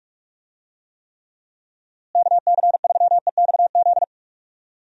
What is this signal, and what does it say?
Morse code, heard as a keyed tone, about 41 wpm, 700 Hz. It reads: KC3EXZ
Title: USA Call Signs - 6 Characters - Repeated 3x - 45wpm